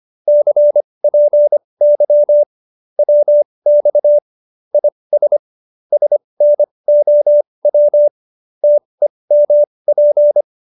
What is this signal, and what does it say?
Morse code, heard as a keyed tone, about 25 wpm, spaced as for 20 wpm, 590 Hz. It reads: CPY WX IS SNOW TEMP
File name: QSO302 20